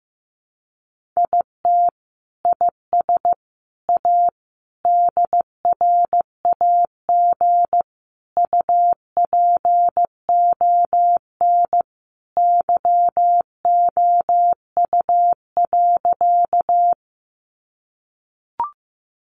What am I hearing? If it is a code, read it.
Morse code, heard as a keyed tone, about 15 wpm, 700 Hz. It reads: IT IS A DRAG UPON YOU.
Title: Sentences from Top 1500 Words 15wpm